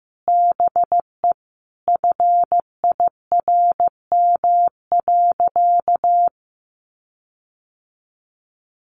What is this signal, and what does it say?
Morse code, heard as a keyed tone, about 15 wpm, 700 Hz. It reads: BE FIRM.
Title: Sentences from Top 1400 Words 15wpm